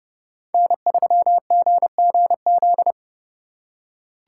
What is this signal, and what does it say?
Morse code, heard as a keyed tone, about 28 wpm, 700 Hz. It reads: N3GGZ